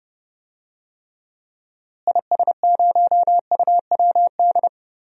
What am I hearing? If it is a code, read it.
Morse code, heard as a keyed tone, about 30 wpm, 700 Hz. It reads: IS0UWD
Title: DXCC-100 Call Signs - 6 Characters - Repeated 2x - 30wpm